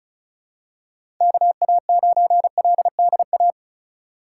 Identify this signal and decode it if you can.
Morse code, heard as a keyed tone, about 34 wpm, 700 Hz. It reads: KA9LDA
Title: USA Call Signs - 4x Character Spacing - 35wpm